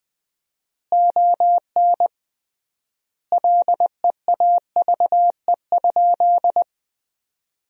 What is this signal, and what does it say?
Morse code, heard as a keyed tone, about 20 wpm, 700 Hz. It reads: ON LEAVE?
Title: 3x Word-Spacing - Sentences from Top 400 Words - 20wpm